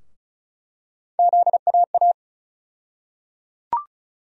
Morse code, heard as a keyed tone, about 34 wpm, 700 Hz. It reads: ZAA
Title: USA Call Sign Postfixes - 3 Letters 35wpm